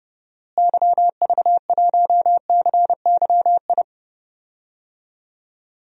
Morse code, heard as a keyed tone, about 31 wpm, 700 Hz. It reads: YV1CYI
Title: DXCC-100 Call Signs - 6 Characters - Repeated 2x - 30wpm